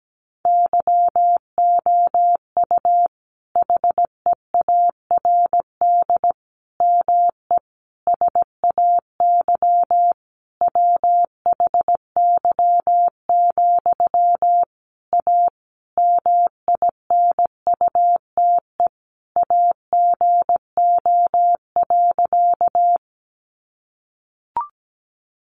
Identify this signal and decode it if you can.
Morse code, heard as a keyed tone, about 17 wpm, 700 Hz. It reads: YOU HEARD ME SAY WHY, A MINUTE AGO.
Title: Sentences from Top 500 Words 17wpm